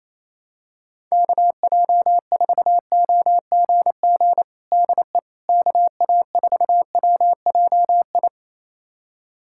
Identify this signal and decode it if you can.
Morse code, heard as a keyed tone, about 28 wpm, 700 Hz. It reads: KJ4OGG DE KA4WJI
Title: USA Callsign DE Callsign - 6 Characters - 28wpm